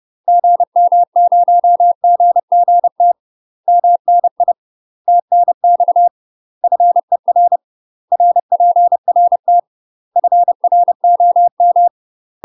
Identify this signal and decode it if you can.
Morse code, heard as a keyed tone, about 30 wpm, 700 Hz. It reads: GM0GGT MNI TNX FER RPRT FROM
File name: UK MASTER QSOs TRAINING / 30WPM / UK MASTER QSOs 0007